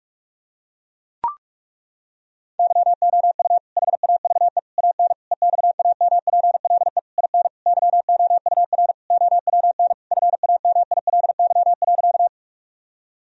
Morse code, heard as a keyed tone, about 42 wpm, 700 Hz. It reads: YOU HAVE AN EXAMPLE IN YOUR OWN FAMILY.
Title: Sentences from Top 1000 Words - Review Part 1 - 45wpm